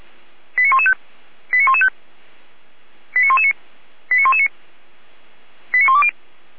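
Five-tone selective call (ZVEI), heard as quick runs of five tones, tones of about 70 ms. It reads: 89115 89115 89119 89119 89120
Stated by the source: ZVEI five-tone call received by SDR, from feuerwehr-orlamuende.org.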